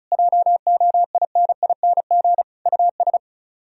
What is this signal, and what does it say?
Morse code, heard as a keyed tone, about 40 wpm, 700 Hz. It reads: JOINING US